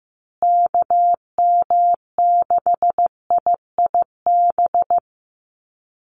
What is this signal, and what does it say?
Morse code, heard as a keyed tone, about 14 wpm, 700 Hz. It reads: KM6IIB
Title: USA Call Signs Repeated 2x 15wpm